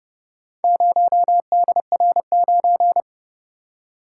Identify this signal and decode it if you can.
Morse code, heard as a keyed tone, about 30 wpm, 700 Hz. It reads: 0DR9